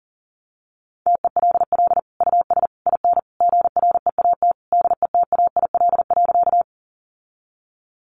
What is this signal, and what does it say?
Morse code, heard as a keyed tone, about 40 wpm, 700 Hz. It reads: TELL US IN GREAT DETAIL.
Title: Sentences from Top 1500 Words 40wpm